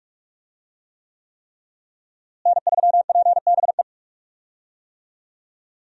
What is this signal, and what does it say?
Morse code, heard as a keyed tone, about 46 wpm, 700 Hz. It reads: N3PBE
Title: USA Call Signs - 5 Characters - Repeated 3x - 45wpm